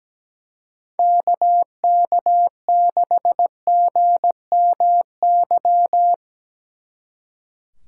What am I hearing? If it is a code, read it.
Morse code, heard as a keyed tone, about 18 wpm, 700 Hz. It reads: KK6GMY
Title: Speed Racing - USA 2x3 Call Signs 17wpm